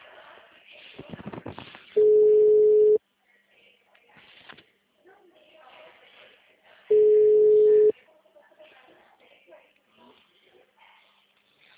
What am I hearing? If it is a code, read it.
Ringback tone (Europe, 425 Hz), with speech.
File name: out- -2011- - - .4